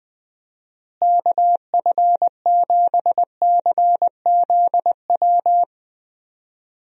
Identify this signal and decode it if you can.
Morse code, heard as a keyed tone, about 19 wpm, 700 Hz. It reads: KF7CZW
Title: USA Call Signs - 6 Characters - Repeated 3x - 20wpm